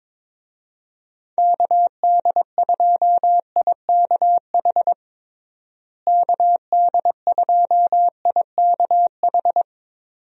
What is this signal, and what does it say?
Morse code, heard as a keyed tone, about 22 wpm, 700 Hz. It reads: KD2IKH KD2IKH
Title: USA Call Signs - 6 Characters - Repeated 2x - 22wpm